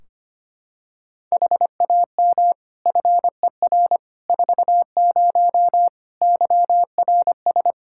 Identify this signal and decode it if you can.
Morse code, heard as a keyed tone, about 25 wpm, 700 Hz. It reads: HAM FER 40 YRS
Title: Practice QSO - Exchange 3 - More Info - 25wpm